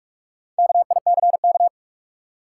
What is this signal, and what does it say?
Morse code, heard as a keyed tone, about 41 wpm, 700 Hz. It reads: KICK
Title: Sentences from Top 1300 Words - New Word Drill - 45wpm